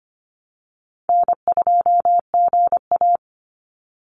Morse code, heard as a keyed tone, about 25 wpm, 700 Hz. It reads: N2GA
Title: Easy Call Signs Repeated 3x - 25wpm